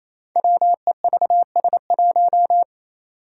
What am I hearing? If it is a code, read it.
Morse code, heard as a keyed tone, about 27 wpm, 700 Hz. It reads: WEVS1
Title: Five Challenging Letter-Number Combinations 28wpm